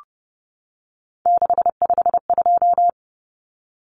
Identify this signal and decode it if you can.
Morse code, heard as a keyed tone, about 30 wpm, 700 Hz. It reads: 652
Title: Numbers - 3-Digit 30wpm